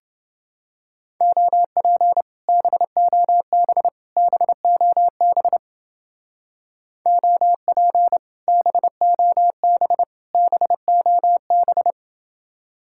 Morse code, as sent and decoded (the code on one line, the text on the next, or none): --- .--. / -... --- -... / -... --- -... / --- .--. / -... --- -... / -... --- -...
OP BOB BOB OP BOB BOB